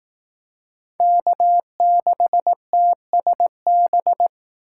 Morse code, heard as a keyed tone, about 18 wpm, 700 Hz. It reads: K6TSB
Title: Speed Racing - USA 1x3 Call Signs 15wpm